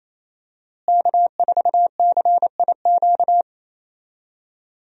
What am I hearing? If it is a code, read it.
Morse code, heard as a keyed tone, about 28 wpm, 700 Hz. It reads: K4CIQ
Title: USA Call Signs - 5 Characters - Repeated 3x - 28wpm